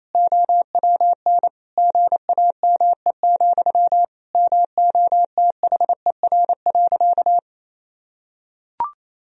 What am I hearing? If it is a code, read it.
Morse code, heard as a keyed tone, about 28 wpm, 700 Hz. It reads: OWN GAME, MOTHER.
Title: Sentences from Top 1000 Words - Review Part 3 - 28wpm